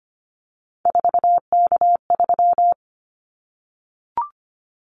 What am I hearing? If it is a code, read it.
Morse code, heard as a keyed tone, about 25 wpm, 700 Hz. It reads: 4K3
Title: Callsign Prefixes for Top 100 DXCC - 25wpm